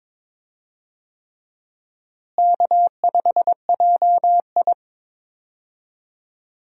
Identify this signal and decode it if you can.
Morse code, heard as a keyed tone, about 22 wpm, 700 Hz. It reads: K5JI
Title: USA Call Signs - 4 Characters - Repeated 3x - 22wpm